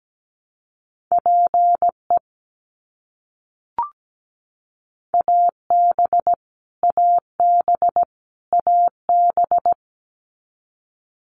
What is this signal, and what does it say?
Morse code, heard as a keyed tone, about 17 wpm, 700 Hz. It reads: PE AB AB AB
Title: Canadian Provinces and Territories - Repeated 3x - 17wpm